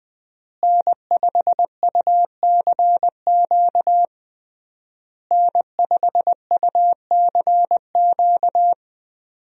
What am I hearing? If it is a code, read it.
Morse code, heard as a keyed tone, about 21 wpm, 700 Hz. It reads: N5UCQ N5UCQ
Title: USA Call Signs - 5 Characters - Repeated 3x - 20wpm